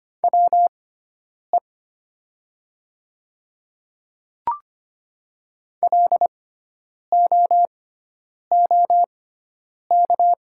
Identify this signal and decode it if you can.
Morse code, heard as a keyed tone, about 25 wpm, spaced as for 9 wpm, 690 Hz. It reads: WE LOOK
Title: 6x Character Spacing - Top 500 Words - 25wpm